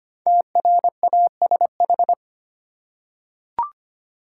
Morse code, heard as a keyed tone, about 25 wpm, 700 Hz. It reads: TRASH
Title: Five Letter Words - 25wpm